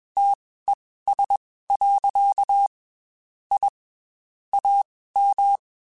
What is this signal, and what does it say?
Morse code, heard as a keyed tone, about 21 wpm, spaced as for 15 wpm, 800 Hz. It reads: TES. I AM
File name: QSOs / QSOs-mp3s / WD0FBT 1@20